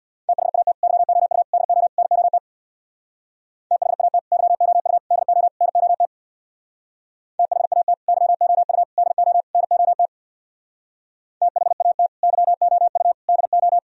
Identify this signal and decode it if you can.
Morse code, heard as a keyed tone, about 64 wpm, 700 Hz. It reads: THAT YOU DO NOT THAT YOU DO NOT THAT YOU DO NOT THAT YOU DO